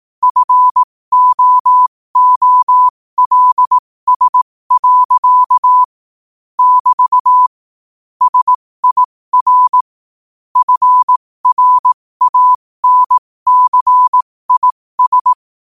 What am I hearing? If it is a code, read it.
Morse code, heard as a keyed tone, about 18 wpm, spaced as for 15 wpm, 1000 Hz. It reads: FOOLS. <BT> SIR FRANCIS
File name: qotdfiles / 15wpm